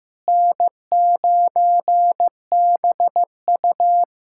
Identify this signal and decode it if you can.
Morse code, heard as a keyed tone, about 15 wpm, 700 Hz. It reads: N9BU
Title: K1USN SST QSO Practice - 15wpm